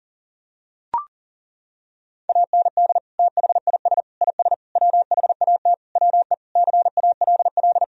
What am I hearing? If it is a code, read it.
Morse code, heard as a keyed tone, about 40 wpm, 700 Hz. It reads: AND THIS IS WHAT WE CALL
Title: Sentences from Top 1000 Words - Review Part 3 - 40wpm